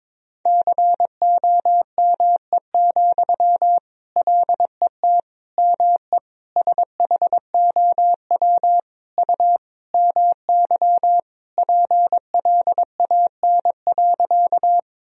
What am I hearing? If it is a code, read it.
Morse code, heard as a keyed tone, about 22 wpm, 700 Hz. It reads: COME, LET ME SHOW U MY PLAN.
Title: Sentences from Top 500 Words and Abbreviations 22wpm